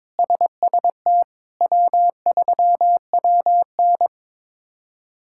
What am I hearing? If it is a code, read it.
Morse code, heard as a keyed tone, about 23 wpm, 700 Hz. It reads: SST W3WN
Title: K1USN SST QSO Practice - 22wpm